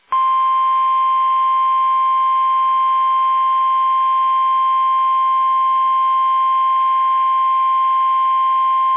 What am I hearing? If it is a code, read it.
Emergency Alert System (SAME) tones (the 1050 Hz attention tone). It is heard from 0.8 to 7.8 s.